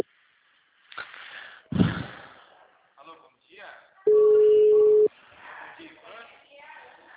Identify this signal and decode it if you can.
Ringback tone (Europe, 425 Hz), with speech.